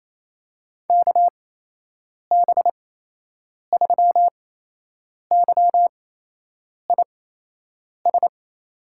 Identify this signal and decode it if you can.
Morse code, heard as a keyed tone, about 28 wpm, spaced as for 8 wpm, 700 Hz. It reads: KB3YIS
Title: USA Call Signs - 8x Character Spacing - 28wpm